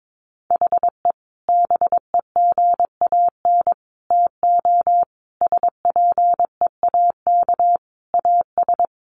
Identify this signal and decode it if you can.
Morse code, heard as a keyed tone, about 22 wpm, 700 Hz. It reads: HE BEGAN TO SPEAK AS
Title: Sentences from Top 1100 Words 22wpm